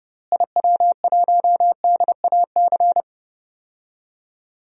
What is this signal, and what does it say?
Morse code, heard as a keyed tone, about 30 wpm, 700 Hz. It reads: IW1DAC
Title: DXCC-100 Call Signs - 6 Characters - Repeated 2x - 30wpm